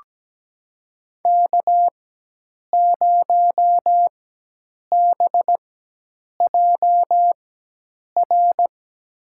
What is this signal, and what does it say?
Morse code, heard as a keyed tone, about 17 wpm, spaced as for 8 wpm, 700 Hz. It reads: K0BJR
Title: USA Call Signs - 4x Character Spacing - 17wpm